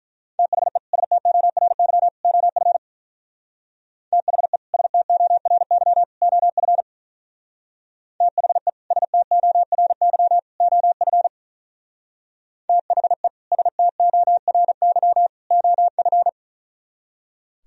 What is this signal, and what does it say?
Morse code, heard as a keyed tone, about 42 wpm, 700 Hz. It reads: THE STORY OF THE STORY OF THE STORY OF THE STORY OF
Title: Speed-Racing - Sets of 3 Words 35wpm